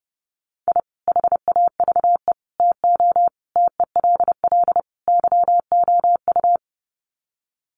Morse code, heard as a keyed tone, about 30 wpm, 700 Hz. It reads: I HAVE TO TELL YOU
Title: Sets of 5 Words 30wpm